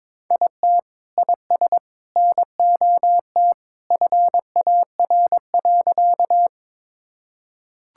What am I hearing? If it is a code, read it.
Morse code, heard as a keyed tone, about 21 wpm, 700 Hz. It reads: IT IS NOT FAR.
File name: practice / 22wpm / Sentences with 3-Word Phrases - 22wpm